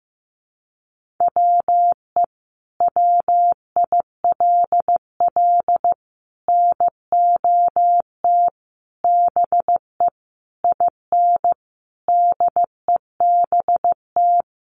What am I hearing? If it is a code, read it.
Morse code, heard as a keyed tone, about 15 wpm, 700 Hz. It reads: WE WILL NOT BE IN DEBT
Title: Sentences from Top 1300 Words 15wpm